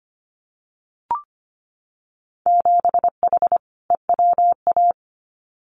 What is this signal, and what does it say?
Morse code, heard as a keyed tone, about 25 wpm, 700 Hz. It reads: 7H EWA
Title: Winter Field Day Exchanges Encore 25wpm